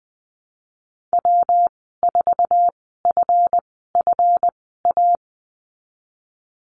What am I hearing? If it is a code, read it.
Morse code, heard as a keyed tone, about 20 wpm, spaced as for 15 wpm, 700 Hz. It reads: W4FFA
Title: 179 Lesson 29g - Slash - Call Signs 20wpm